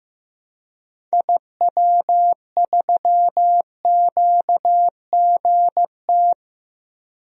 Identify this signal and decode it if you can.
Morse code, heard as a keyed tone, about 15 wpm, 690 Hz. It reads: IW3QGT
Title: DXCC-100 Call Signs - 6 Characters - 15wpm